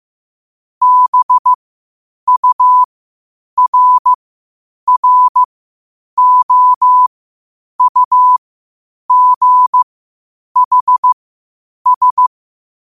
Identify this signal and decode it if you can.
Morse code, heard as a keyed tone, about 15 wpm, spaced as for 8 wpm, 1000 Hz. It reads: BURROUGHS
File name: qotdfiles / 8wpm